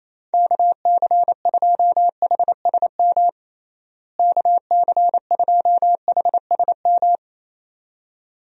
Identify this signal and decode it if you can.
Morse code, heard as a keyed tone, about 28 wpm, 700 Hz. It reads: KC2HSM KC2HSM
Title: USA Call Signs - 6 Characters - Repeated 2x - 28wpm